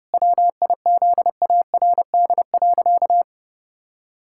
Morse code, heard as a keyed tone, about 30 wpm, 700 Hz. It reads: WIZARD.
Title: Pangram Challenge v2 - 30wpm